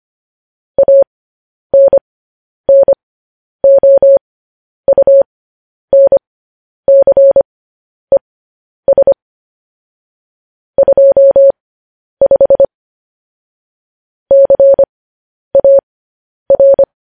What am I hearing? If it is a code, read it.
Morse code, heard as a keyed tone, about 25 wpm, spaced as for 10 wpm, 550 Hz. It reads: ANNOUNCES 25 CAR
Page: news-headlines-in-morse-code-at-10-wpm